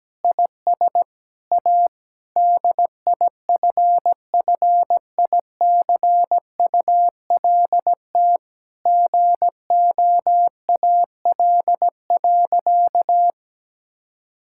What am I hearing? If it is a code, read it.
Morse code, heard as a keyed tone, about 17 wpm, 700 Hz. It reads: IS A DIFFICULT GOAL.
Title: Sentences from Top 1000 Words - Review Part 5 - 17wpm